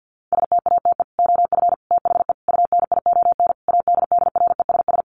Morse code, heard as a keyed tone, about 50 wpm, 700 Hz. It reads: STATE OF THE UNION ADDRESS
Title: Sets of 5 Words Encore 50wpm